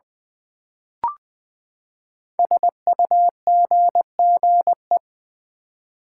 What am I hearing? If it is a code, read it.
Morse code, heard as a keyed tone, about 20 wpm, 700 Hz. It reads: SUGGE
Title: Build A Word - 5 Letters At A Time - 10-Letter Words - 20wpm